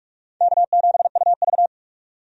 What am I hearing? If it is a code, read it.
Morse code, heard as a keyed tone, about 46 wpm, 700 Hz. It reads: K7UV
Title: USA Call Signs - 4 Characters - 45wpm